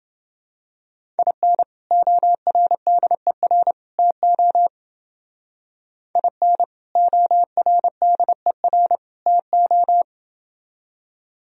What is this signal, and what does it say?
Morse code, heard as a keyed tone, about 28 wpm, 700 Hz. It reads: IN ORDER TO IN ORDER TO